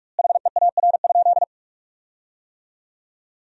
Morse code, heard as a keyed tone, about 43 wpm, 700 Hz. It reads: HEAR?
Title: 4x Word-Spacing - Sentences from Top 300 Words - 45wpm